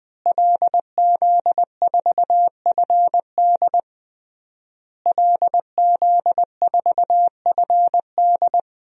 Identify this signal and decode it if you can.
Morse code, heard as a keyed tone, about 20 wpm, 700 Hz. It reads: LZ4FD LZ4FD
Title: DXCC-100 Call Signs - 5 Characters - Repeated 3x - 20wpm